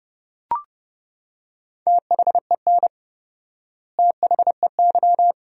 Morse code, heard as a keyed tone, about 31 wpm, 700 Hz. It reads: THEN THEY
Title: 4x Word-Spacing - Sentences from Top 200 Words - 30wpm